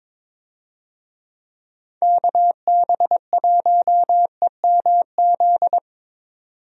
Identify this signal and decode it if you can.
Morse code, heard as a keyed tone, about 22 wpm, 700 Hz. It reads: KB1EMZ
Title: USA Call Signs - 6 Characters - 22wpm